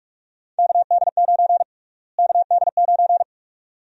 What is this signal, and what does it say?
Morse code, heard as a keyed tone, about 45 wpm, 700 Hz. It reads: KD9 KD9
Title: USA Call Sign Prefixes - Repeated 2x - 45wpm